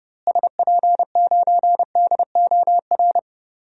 Morse code, heard as a keyed tone, about 30 wpm, 700 Hz. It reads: SP9DOR
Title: DXCC-100 Call Signs - 6 Characters - Repeated 3x - 30wpm